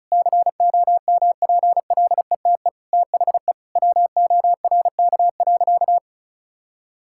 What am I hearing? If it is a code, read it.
Morse code, heard as a keyed tone, about 36 wpm, 700 Hz. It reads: COMPLETE THE WORK.
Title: Sentences from Top 1000 Words - Review Part 6 - 35wpm